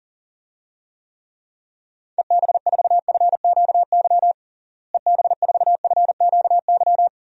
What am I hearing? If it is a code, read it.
Morse code, heard as a keyed tone, about 40 wpm, 700 Hz. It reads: EB4FQY EB4FQY
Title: DXCC-100 Call Signs - 6 Characters - Repeated 3x - 40wpm